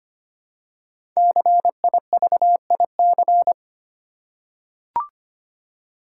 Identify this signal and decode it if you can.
Morse code, heard as a keyed tone, about 25 wpm, 700 Hz. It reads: CIVIC
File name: 5 Letter Words - First 3 Letters Given - 25wpm